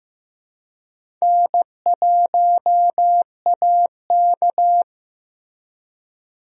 Morse code, heard as a keyed tone, about 15 wpm, 700 Hz. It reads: N1AK
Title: USA Call Signs - 4 Characters - Repeated 3x - 15wpm